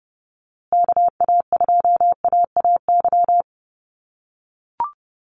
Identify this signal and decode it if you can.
Morse code, heard as a keyed tone, about 30 wpm, 700 Hz. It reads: KA2AAY
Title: USA Call Signs Repeated 3x 30wpm